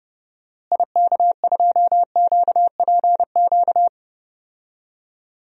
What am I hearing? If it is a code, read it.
Morse code, heard as a keyed tone, about 30 wpm, 700 Hz. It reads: IK2QPQ
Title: DXCC-100 Call Signs - 6 Characters - Repeated 2x - 30wpm